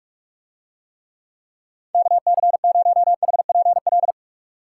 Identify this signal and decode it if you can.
Morse code, heard as a keyed tone, about 41 wpm, 700 Hz. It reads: KC0HPL